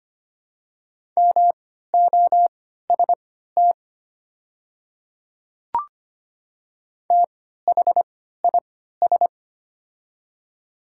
Morse code, heard as a keyed tone, about 25 wpm, spaced as for 14 wpm, 690 Hz. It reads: MOST THIS
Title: 3x Character Spacing - Top 100 Words - 25wpm